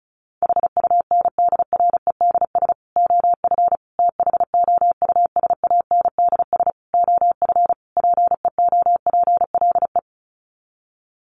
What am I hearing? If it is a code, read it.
Morse code, heard as a keyed tone, about 34 wpm, 700 Hz. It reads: HUNDREDS OF THOUSANDS OF PEOPLE
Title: Sets of 5 Words Encore2 35wpm